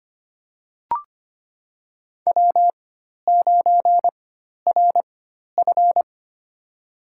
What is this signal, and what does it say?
Morse code, heard as a keyed tone, about 25 wpm, spaced as for 12 wpm, 700 Hz. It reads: W9RF
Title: USA Call Signs - 4 Characters - 4x Character Spacing - 25wpm